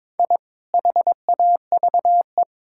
Morse code, heard as a keyed tone, about 22 wpm, 700 Hz. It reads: I HAVE
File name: practice / 22wpm / Sentences with 2-Word Phrases - 22wpm